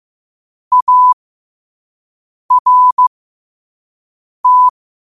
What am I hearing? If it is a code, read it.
Morse code, heard as a keyed tone, about 15 wpm, spaced as for 5 wpm, 1000 Hz. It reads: ART
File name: qotdfiles / 5wpm